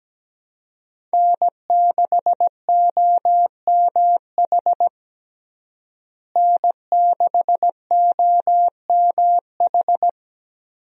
Morse code, heard as a keyed tone, about 17 wpm, 700 Hz. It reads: N6OMH N6OMH
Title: USA Call Signs - 5 Characters - Repeated 3x - 17wpm